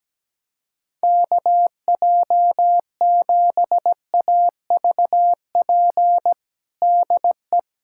Morse code, heard as a keyed tone, about 17 wpm, 700 Hz. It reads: KJ7AVP DE
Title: USA Callsign DE Callsign 17wpm